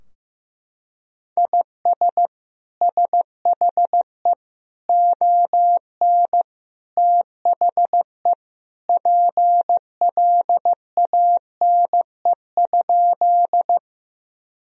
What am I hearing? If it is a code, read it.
Morse code, heard as a keyed tone, about 15 wpm, 700 Hz. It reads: IS SHE ON THE PLANE?
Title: Sentences from Top 1000 Words - Review Part 3 - 15wpm